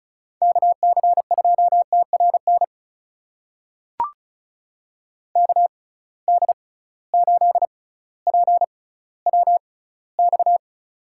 Morse code, heard as a keyed tone, about 36 wpm, spaced as for 18 wpm, 700 Hz. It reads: KC2TRN KD8PWX
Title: USA Call Signs - 6 Characters - 6x Character Spacing - 35wpm